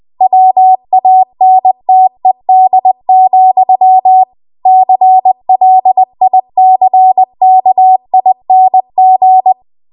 Morse code, heard as a keyed tone, about 20 wpm, 750 Hz. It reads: WANTED, CLICKING
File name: Archive / 20 WPM / 20WPM